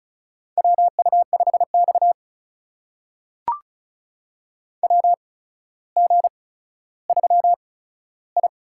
Morse code, heard as a keyed tone, about 36 wpm, spaced as for 15 wpm, 700 Hz. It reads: WU5X WG3I
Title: USA Call Signs - 4 Characters - 8x Character Spacing - 35wpm